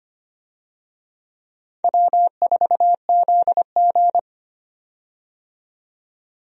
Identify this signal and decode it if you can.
Morse code, heard as a keyed tone, about 25 wpm, 700 Hz. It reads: W4ZG
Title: USA Call Signs - 4 Characters - 25wpm